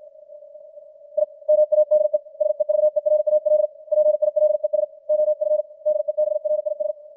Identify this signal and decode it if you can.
Morse code, heard as a keyed tone, about 56 wpm, 610 Hz. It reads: I MADE SEVERAL PILES OF DEBRIS
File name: archived / CWBros